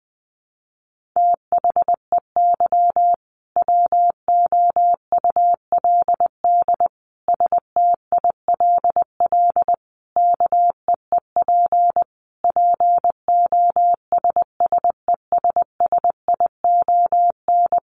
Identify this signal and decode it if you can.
Morse code, heard as a keyed tone, about 20 wpm, 700 Hz. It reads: THEY WOULD STILL KEEP POSSESSION
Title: Sentences from Top 1400 Words 20wpm